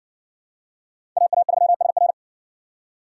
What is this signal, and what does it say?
Morse code, heard as a keyed tone, about 60 wpm, 700 Hz. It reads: AA3SR